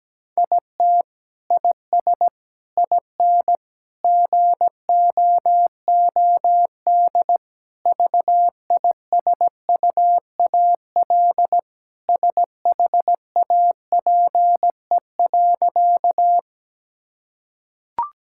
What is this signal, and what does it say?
Morse code, heard as a keyed tone, about 17 wpm, 700 Hz. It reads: IT IS IN GOOD VISUAL SHAPE.
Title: Sentences from Top 1000 Words - Review Part 6 - 17wpm